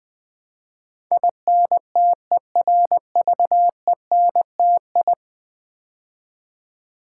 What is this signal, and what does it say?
Morse code, heard as a keyed tone, about 20 wpm, 700 Hz. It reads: INTERVENTI
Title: Build A Word - 5 Letters At A Time - 12-Letter Words - 20wpm